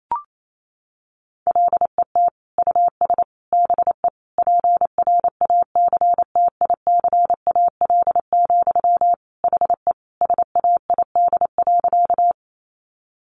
Morse code, heard as a keyed tone, about 28 wpm, 700 Hz. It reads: LET US BE PRACTICAL, HE SAID.
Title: Sentences from Top 1400 Words 28wpm